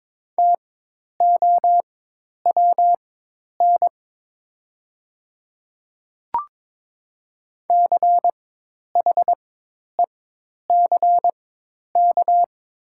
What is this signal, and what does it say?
Morse code, heard as a keyed tone, about 23 wpm, spaced as for 10 wpm, 700 Hz. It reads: TOWN CHECK
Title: 4x Character Spacing - Top 500 Words - 22wpm